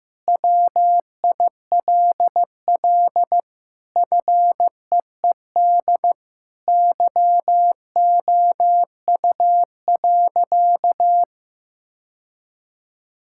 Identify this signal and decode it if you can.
Morse code, heard as a keyed tone, about 15 wpm, 700 Hz. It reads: WILL FEED YOU.
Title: Sentences from Top 1600 Words 15wpm